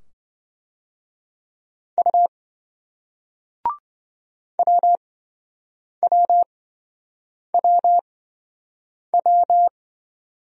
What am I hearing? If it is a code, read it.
Morse code, heard as a keyed tone, about 27 wpm, spaced as for 7 wpm, 700 Hz. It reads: U WWWW